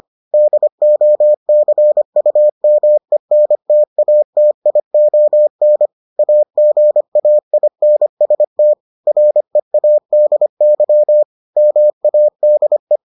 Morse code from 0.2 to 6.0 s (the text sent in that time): DOCUMENTATION